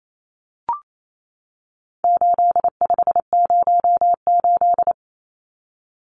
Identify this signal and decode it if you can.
Morse code, heard as a keyed tone, about 28 wpm, 700 Hz. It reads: 8508
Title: Numbers - 4-Digit 28wpm